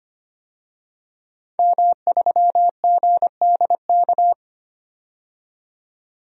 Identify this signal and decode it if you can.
Morse code, heard as a keyed tone, about 25 wpm, 700 Hz. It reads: M3GDK
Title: DXCC-100 Call Signs - 5 Characters - 25wpm